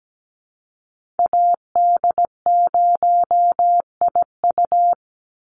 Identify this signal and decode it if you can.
Morse code, heard as a keyed tone, about 17 wpm, 700 Hz. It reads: AD0IU
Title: Call Signs - 17wpm